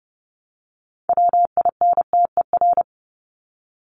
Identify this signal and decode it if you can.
Morse code, heard as a keyed tone, about 30 wpm, 700 Hz. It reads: WINTER
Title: Top 1501-1750 Words 30wpm